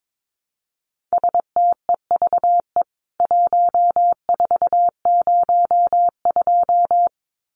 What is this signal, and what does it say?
Morse code, heard as a keyed tone, about 22 wpm, 700 Hz. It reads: STEVE 1402
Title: CWT Exchange 22wpm